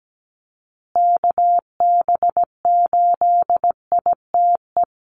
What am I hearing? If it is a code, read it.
Morse code, heard as a keyed tone, about 17 wpm, 700 Hz. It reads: KB8ITE
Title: USA Call Signs Repeated 2x 17wpm